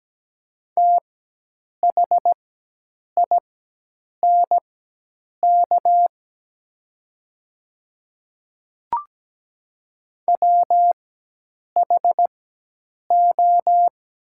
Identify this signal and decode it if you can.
Morse code, heard as a keyed tone, about 17 wpm, spaced as for 8 wpm, 700 Hz. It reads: THINK WHO
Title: 4x Character Spacing - Top 200 Words - 17wpm